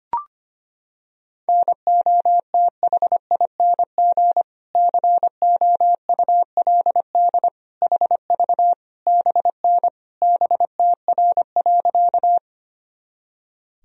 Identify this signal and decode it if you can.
Morse code, heard as a keyed tone, about 25 wpm, 700 Hz. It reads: NOTHING COULD HV BN BTR.
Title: Sentences from Top 500 Words and Abbreviations 25wpm